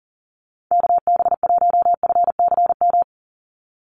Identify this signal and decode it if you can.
Morse code, heard as a keyed tone, about 40 wpm, 700 Hz. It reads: KB1FCM
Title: USA Call Signs 40wpm